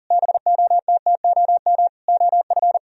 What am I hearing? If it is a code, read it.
Morse code, heard as a keyed tone, about 50 wpm, 700 Hz. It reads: BOTTOM OF
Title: Speed-Racing - Sets of 4 Words 40wpm